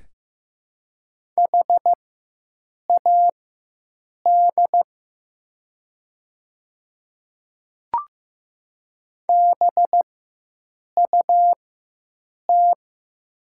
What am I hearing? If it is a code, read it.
Morse code, heard as a keyed tone, about 15 wpm, spaced as for 7 wpm, 700 Hz. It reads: HAD BUT
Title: 4x Character Spacing - Top 300 Words - 15wpm